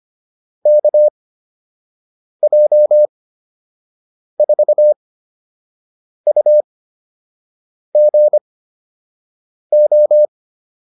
Morse code, heard as a keyed tone, about 25 wpm, spaced as for 6 wpm, 600 Hz. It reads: KJ4UGO